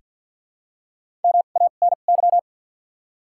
Morse code, heard as a keyed tone, about 49 wpm, spaced as for 36 wpm, 700 Hz. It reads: MANY
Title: 2x Character Spacing - Top 200 Words - 50wpm